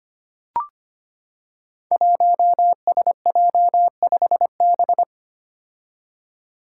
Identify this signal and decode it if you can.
Morse code, heard as a keyed tone, about 25 wpm, 700 Hz. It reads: 1SJ5B
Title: Five Challenging Letter-Number Combinations 25wpm